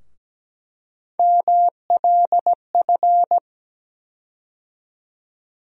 Morse code, heard as a keyed tone, about 17 wpm, 700 Hz. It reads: MLF
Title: USA Call Sign Postfixes - 3 Letters 17wpm